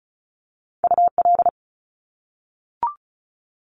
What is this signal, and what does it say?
Morse code, heard as a keyed tone, about 36 wpm, 700 Hz. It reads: UL
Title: Top Two Letter Combinations in Words 35wpm